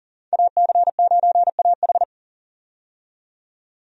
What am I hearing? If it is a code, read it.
Morse code, heard as a keyed tone, about 40 wpm, 700 Hz. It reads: AC9AH
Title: USA Call Signs - 5 Characters - 40wpm